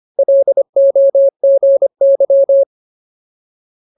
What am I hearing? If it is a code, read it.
Morse code, heard as a keyed tone, about 25 wpm, 550 Hz. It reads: LOGY
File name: Ultimate Word List / 25WPM Ultimate